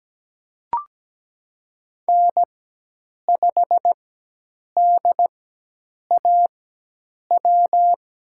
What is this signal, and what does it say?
Morse code, heard as a keyed tone, about 17 wpm, spaced as for 8 wpm, 700 Hz. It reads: N5DAW
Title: USA Call Signs - 5 Characters - 4x Character Spacing - 17wpm